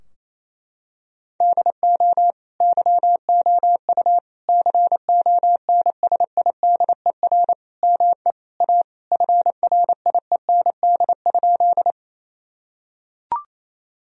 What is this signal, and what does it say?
Morse code, heard as a keyed tone, about 28 wpm, 700 Hz. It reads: DO YOU CONSIDER ME A FRIEND?
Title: Sentences from Top 1000 Words - Review Part 4 - 28wpm